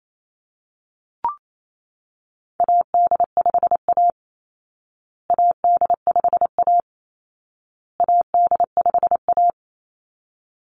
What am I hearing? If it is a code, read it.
Morse code, heard as a keyed tone, about 28 wpm, 700 Hz. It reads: AD5A AD5A AD5A
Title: Call Signs Repeated 2x - 28wpm